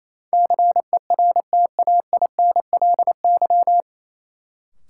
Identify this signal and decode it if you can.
Morse code, heard as a keyed tone, about 28 wpm, 700 Hz. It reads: CERTAINLY
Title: Sentences from Top 900 Words - New Word Drill - 28wpm